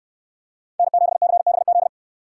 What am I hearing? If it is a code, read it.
Morse code, heard as a keyed tone, about 58 wpm, 700 Hz. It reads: N7CBZ